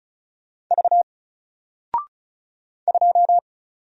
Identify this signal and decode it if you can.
Morse code, heard as a keyed tone, about 35 wpm, 700 Hz. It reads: V 2
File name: Single-Letter-Number-35wpm